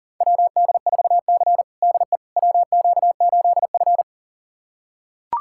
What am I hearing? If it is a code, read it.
Morse code, heard as a keyed tone, about 40 wpm, 700 Hz. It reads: WD4C DE WQ8F
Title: USA Callsign DE Callsign - 4 Characters - 40wpm